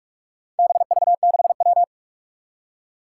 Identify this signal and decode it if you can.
Morse code, heard as a keyed tone, about 44 wpm, 700 Hz. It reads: BV6W
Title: Four Challenging Letter-Number Combinations 45wpm